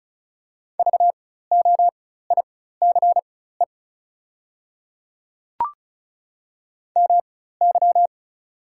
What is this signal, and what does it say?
Morse code, heard as a keyed tone, about 34 wpm, spaced as for 16 wpm, 690 Hz. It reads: VOICE MY
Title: 4x Character Spacing - Top 500 Words - 35wpm